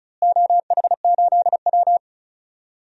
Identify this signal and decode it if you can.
Morse code, heard as a keyed tone, about 34 wpm, 700 Hz. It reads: OH8W
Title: DXCC-100 Call Signs - 4 Characters - Repeated 2x - 35wpm